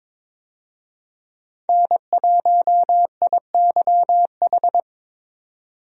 Morse code, heard as a keyed tone, about 22 wpm, 700 Hz. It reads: N1IYH
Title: USA Call Signs - 5 Characters - 22wpm